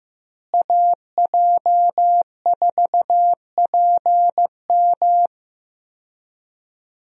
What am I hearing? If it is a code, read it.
Morse code, heard as a keyed tone, about 15 wpm, 700 Hz. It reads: AJ4PM
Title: USA Call Signs - 5 Characters - Repeated 3x - 15wpm